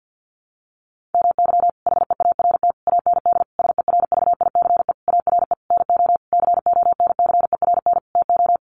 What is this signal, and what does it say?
Morse code, heard as a keyed tone, about 50 wpm, 700 Hz. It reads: MY HEART AND SERVICE ARE NO CONCERN TO
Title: Sentences from Top 1200 Words 50wpm